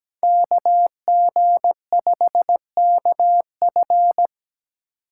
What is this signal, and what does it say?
Morse code, heard as a keyed tone, about 20 wpm, 700 Hz. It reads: KG5KF
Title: Speed Racing - USA 2x2 Call Signs 17wpm